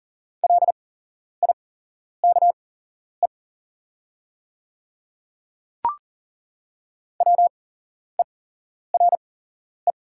Morse code, heard as a keyed tone, about 40 wpm, spaced as for 11 wpm, 700 Hz. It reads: LIKE WERE
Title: 8x Character Spacing - Top 200 Words - 40wpm